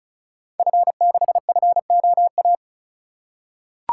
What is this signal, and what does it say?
Morse code, heard as a keyed tone, about 36 wpm, 690 Hz. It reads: F6FOA
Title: Code Groups - 5 Letter:Numbers - 35wpm